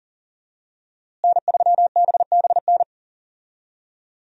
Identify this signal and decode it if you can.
Morse code, heard as a keyed tone, about 40 wpm, 700 Hz. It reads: N3BBN
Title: USA Call Signs - 5 Characters - Repeated 2x - 40wpm